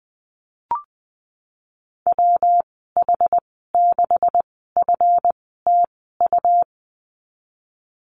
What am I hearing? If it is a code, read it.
Morse code, heard as a keyed tone, about 20 wpm, spaced as for 14 wpm, 700 Hz. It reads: WH6FTU
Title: 179 Lesson 29g - Slash - Call Signs 20wpm